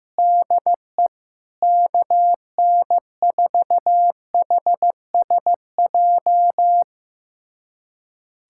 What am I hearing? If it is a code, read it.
Morse code, heard as a keyed tone, about 15 wpm, 700 Hz. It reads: DE KN4HSJ
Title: USA Callsign DE Callsign 15wpm